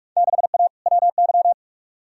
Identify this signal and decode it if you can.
Morse code, heard as a keyed tone, about 41 wpm, 700 Hz. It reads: 6A WY
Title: Field Day Exchanges Encore - 45wpm